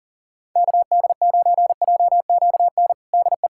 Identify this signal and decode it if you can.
Morse code, heard as a keyed tone, about 40 wpm, 700 Hz. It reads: KD9JQN DE
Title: K1USN SST QSO Practice 40wpm